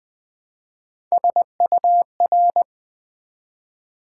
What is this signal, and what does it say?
Morse code, heard as a keyed tone, about 20 wpm, 700 Hz. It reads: SUR
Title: Build A Word - 1 Letter At A Time - 4-Letter Words - 20wpm